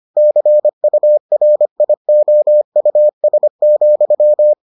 Morse code, heard as a keyed tone, about 25 wpm, 600 Hz. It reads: CURIOUS,